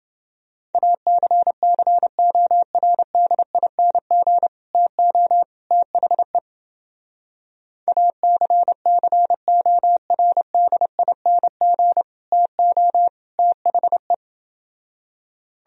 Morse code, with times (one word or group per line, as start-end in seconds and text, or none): 0.7-4.5 ACCORDING
4.7-5.4 TO
5.7-6.4 THE
7.9-12.0 ACCORDING
12.3-13.1 TO
13.4-14.1 THE